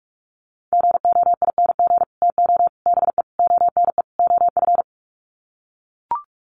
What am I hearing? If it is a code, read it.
Morse code, heard as a keyed tone, about 44 wpm, 690 Hz. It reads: GOING TO BE ONE OF
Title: Sets of 5 Words Encore2 45wpm